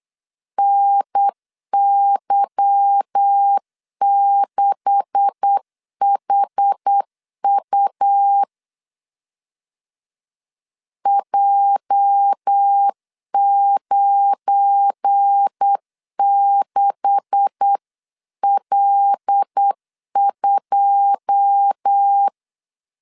Morse code, heard as a keyed tone, about 8 wpm, 800 Hz. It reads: NY6HU J96L2
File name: CW / DET / 1 9 1500C S B 8wpm 5624